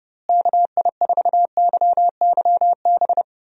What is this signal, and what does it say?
Morse code, heard as a keyed tone, about 30 wpm, 700 Hz. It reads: KI4YYB
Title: USA Call Signs - 6 Characters - Repeated 3x - 30wpm